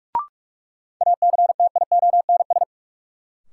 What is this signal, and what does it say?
Morse code, heard as a keyed tone, about 44 wpm, 700 Hz. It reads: ACTIONS
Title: Sentences from Top 1000 Words - Review Part 6 - New Word Drill - 45wpm